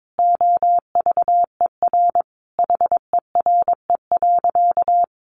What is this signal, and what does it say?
Morse code, heard as a keyed tone, about 22 wpm, 700 Hz. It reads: OVER HERE.
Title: Sentences from Top 1500 Words 22wpm